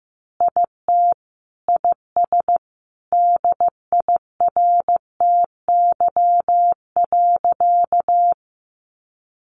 Morse code, heard as a keyed tone, about 15 wpm, 700 Hz. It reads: IT IS DIRTY.
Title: Additional Easy Phrases 15wpm